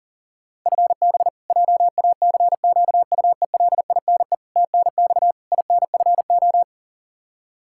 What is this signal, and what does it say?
Morse code, heard as a keyed tone, about 37 wpm, 700 Hz. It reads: FB JACQUELINE TNX INFO
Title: Practice QSO - Exchange 3 - More Info - 40wpm